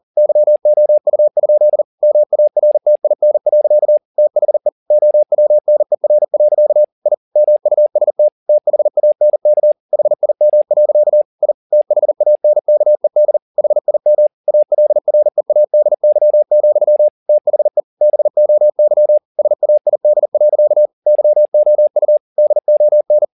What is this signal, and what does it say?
Morse code, heard as a keyed tone, about 40 wpm, 600 Hz. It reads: YOU? MARTIN. THE OWNER. I MUST THANK HIM. I THANKED HIM ALREADY, THE BOY SAID. YOU DON